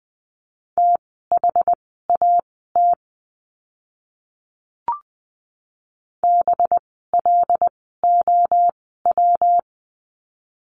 Morse code, heard as a keyed tone, about 20 wpm, spaced as for 15 wpm, 700 Hz. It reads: THAT BLOW
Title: 185 Lesson 30f - Letter b - Words 20wpm